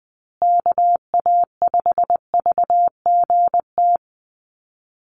Morse code, heard as a keyed tone, about 20 wpm, 700 Hz. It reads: KA5VGT
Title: USA Call Signs Repeated 3x 20wpm